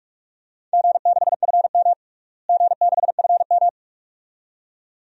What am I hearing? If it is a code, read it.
Morse code, heard as a keyed tone, about 44 wpm, 700 Hz. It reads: G6FM G6FM
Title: DXCC-100 Call Signs - 4 Characters - Repeated 2x - 45wpm